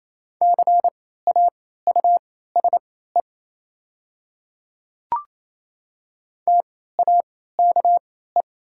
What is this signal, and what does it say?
Morse code, heard as a keyed tone, about 28 wpm, spaced as for 16 wpm, 700 Hz. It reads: CAUSE TAKE
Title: 3x Character Spacing - Top 200 Words - 28wpm